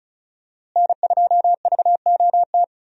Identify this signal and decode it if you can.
Morse code, heard as a keyed tone, about 36 wpm, 700 Hz. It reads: N2VOT
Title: USA Call Signs - 5 Characters - Repeated 3x - 35wpm